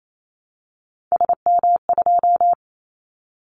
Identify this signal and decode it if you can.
Morse code, heard as a keyed tone, about 28 wpm, 700 Hz. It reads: SM2
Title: Callsign Prefixes for Top 25 DXCC - Repeated 3x - 28wpm